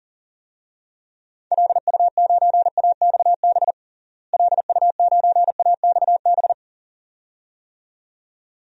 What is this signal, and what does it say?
Morse code, heard as a keyed tone, about 40 wpm, 700 Hz. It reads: LU9AXB LU9AXB
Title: DXCC-100 Call Signs - 6 Characters - Repeated 2x - 40wpm